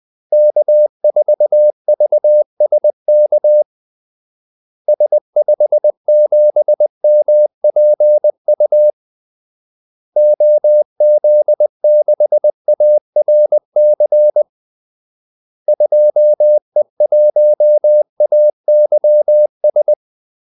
Morse code, heard as a keyed tone, about 20 wpm, 600 Hz. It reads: K4VSK S57MPU OZ6ARC 2E1AYS
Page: Kaldesignaler 20 wpm